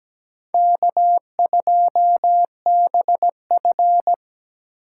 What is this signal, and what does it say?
Morse code, heard as a keyed tone, about 17 wpm, 700 Hz. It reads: K2BF
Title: USA Call Signs - 3x Character Spacing - 17wpm